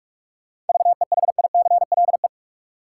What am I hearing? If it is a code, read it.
Morse code, heard as a keyed tone, about 46 wpm, 700 Hz. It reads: VEHICLE
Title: Seven Letter Words - 45wpm